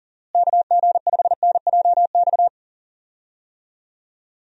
Morse code, heard as a keyed tone, about 40 wpm, 700 Hz. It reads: KG5NJX
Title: USA Call Signs - 6 Characters - Repeated 2x - 40wpm